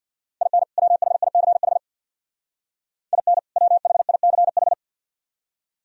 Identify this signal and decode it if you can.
Morse code, heard as a keyed tone, about 62 wpm, 700 Hz. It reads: IN WHICH IN WHICH
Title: Speed-Racing - Sets of 2 Words 50wpm